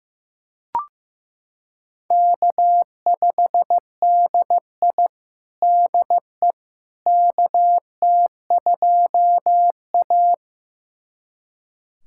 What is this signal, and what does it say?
Morse code, heard as a keyed tone, about 16 wpm, 700 Hz. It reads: K5DI DE KT2A
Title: USA Callsign DE Callsign - 4 Characters - 15wpm